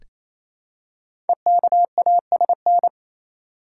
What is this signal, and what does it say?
Morse code, heard as a keyed tone, about 27 wpm, 700 Hz. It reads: EKASN
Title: Code Groups - 5 Letters - 28wpm